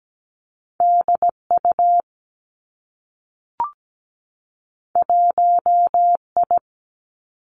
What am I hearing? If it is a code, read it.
Morse code, heard as a keyed tone, about 17 wpm, 700 Hz. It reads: DU 1I
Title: Two Challenging Letter-Number Combinations 17wpm